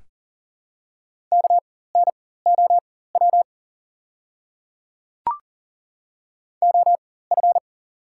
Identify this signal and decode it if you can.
Morse code, heard as a keyed tone, about 37 wpm, spaced as for 18 wpm, 700 Hz. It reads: KNOW OF
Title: 4x Character Spacing - Top 200 Words - 40wpm